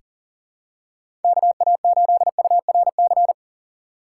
Morse code, heard as a keyed tone, about 39 wpm, 700 Hz. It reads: KA8URC
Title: USA Call Signs - 6 Characters - 3x Character Spacing - 40wpm